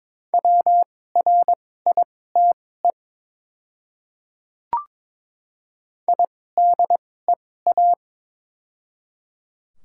Morse code, heard as a keyed tone, about 22 wpm, spaced as for 16 wpm, 700 Hz. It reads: WRITE IDEA
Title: 2x Character Spacing - Top 500 Words - 22wpm